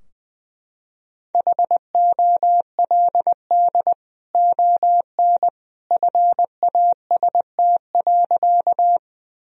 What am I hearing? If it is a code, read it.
Morse code, heard as a keyed tone, about 21 wpm, 700 Hz. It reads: HOLD ON FAST.
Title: Sentences from Top 1000 Words - Review Part 2 - 20wpm